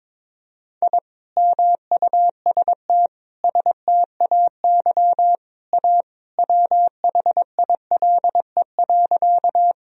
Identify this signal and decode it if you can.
Morse code, heard as a keyed tone, about 22 wpm, 700 Hz. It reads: I MUST STAY A WHILE.
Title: Sentences from Top 1000 Words - Review Part 1 - 22wpm